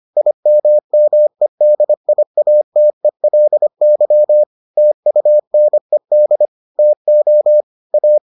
Morse code, heard as a keyed tone, about 25 wpm, 600 Hz. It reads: IMMEDIATELY TUNED TO A